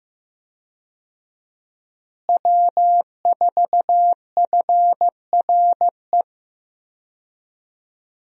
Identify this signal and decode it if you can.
Morse code, heard as a keyed tone, about 15 wpm, 700 Hz. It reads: W4FRE
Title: USA Call Signs - 5 Characters - Repeated 3x - 15wpm